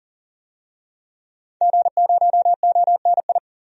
Morse code, heard as a keyed tone, about 40 wpm, 700 Hz. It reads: G0ONI
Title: DXCC-100 Call Signs - 5 Characters - 40wpm